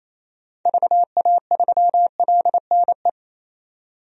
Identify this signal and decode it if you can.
Morse code, heard as a keyed tone, about 28 wpm, 700 Hz. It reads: VA3LNE
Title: DXCC-100 Call Signs - 6 Characters - 28wpm